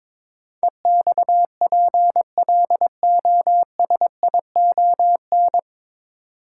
Morse code, heard as a keyed tone, about 22 wpm, 700 Hz. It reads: EXPLOSION
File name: 9 Letter Words - First 3 Letters Given - 22wpm